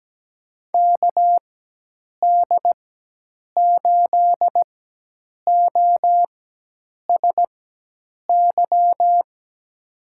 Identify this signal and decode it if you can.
Morse code, heard as a keyed tone, about 17 wpm, spaced as for 8 wpm, 700 Hz. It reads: KD8OSY
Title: USA Call Signs - 6 Characters - 4x Character Spacing - 17wpm